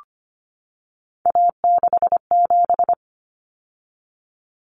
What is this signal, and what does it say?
Morse code, heard as a keyed tone, about 25 wpm, 700 Hz. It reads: A67
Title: Common Cut Numbers - 3-Digit 25wpm